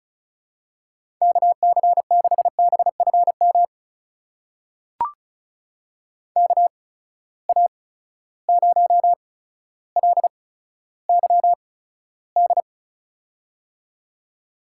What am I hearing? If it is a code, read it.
Morse code, heard as a keyed tone, about 36 wpm, spaced as for 15 wpm, 700 Hz. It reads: KC6BFM KA0LYD
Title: USA Call Signs - 6 Characters - 8x Character Spacing - 35wpm